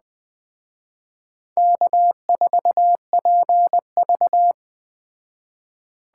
Morse code, heard as a keyed tone, about 21 wpm, 700 Hz. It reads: K4PV
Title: Speed Racing - USA 1x2 Call Signs 17wpm